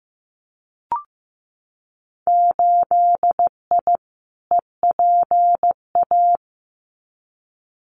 Morse code, heard as a keyed tone, about 15 wpm, 700 Hz. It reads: 8I EPA
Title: Winter Field Day Exchanges Encore 15wpm